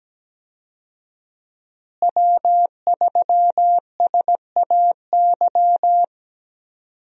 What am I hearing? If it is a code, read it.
Morse code, heard as a keyed tone, about 17 wpm, 700 Hz. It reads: W3SAY
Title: USA Call Signs - 5 Characters - Repeated 2x - 17wpm